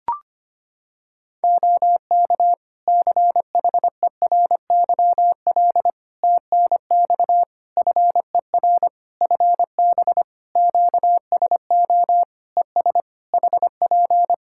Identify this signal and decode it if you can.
Morse code, heard as a keyed tone, about 25 wpm, 700 Hz. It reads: OK CHERYL TNX FER FB QSO ES HP
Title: Practice QSO - Exchange 4 - Ending - 25wpm